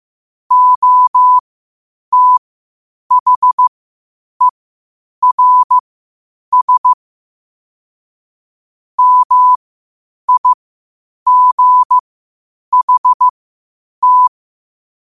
Morse code, heard as a keyed tone, about 15 wpm, spaced as for 8 wpm, 1000 Hz. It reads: OTHERS MIGHT
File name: qotdfiles / 8wpm